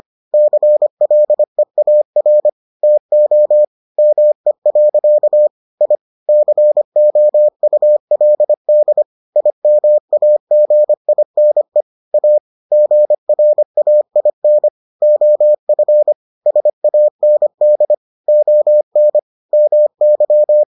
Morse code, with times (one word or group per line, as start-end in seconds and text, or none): 0.3-2.5 CLEAR
2.8-3.6 TO
4.0-5.5 ME.
5.8-6.0 I
6.3-9.0 COULD
9.4-11.8 IMAGINE
12.1-12.4 A
12.7-14.7 GRAIN
15.0-16.1 OF
16.5-17.9 SAND
18.3-19.2 ON
19.5-20.6 MY